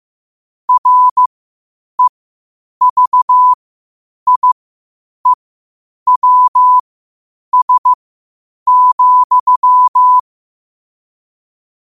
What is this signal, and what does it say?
Morse code, heard as a keyed tone, about 15 wpm, spaced as for 8 wpm, 1000 Hz. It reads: REVIEWS,
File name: qotdfiles / 8wpm